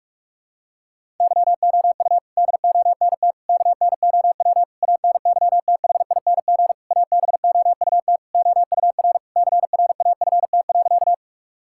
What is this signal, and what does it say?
Morse code, heard as a keyed tone, about 46 wpm, 700 Hz. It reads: YOU DONT KNOW ANYTHING ABOUT OUR CRAFT.
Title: Sentences from Top 1000 Words - Review Part 6 - 45wpm